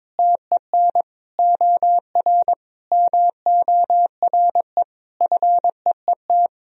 Morse code, heard as a keyed tone, about 21 wpm, 700 Hz. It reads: TEN OR MORE FEET
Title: Sentences from Top 1000 Words - Review Part 3 - 22wpm